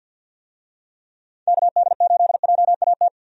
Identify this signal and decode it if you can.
Morse code, heard as a keyed tone, about 50 wpm, 700 Hz. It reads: KD8JAT
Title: USA Call Signs - 6 Characters - 50wpm